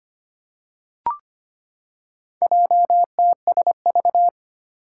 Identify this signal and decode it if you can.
Morse code, heard as a keyed tone, about 25 wpm, 700 Hz. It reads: JTSV
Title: Four Challenging Letter-Number Combinations 25wpm